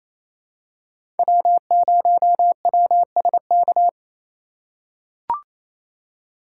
Morse code, heard as a keyed tone, about 28 wpm, 700 Hz. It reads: W0WSK
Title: USA Call Signs - 4x Character Spacing - 28wpm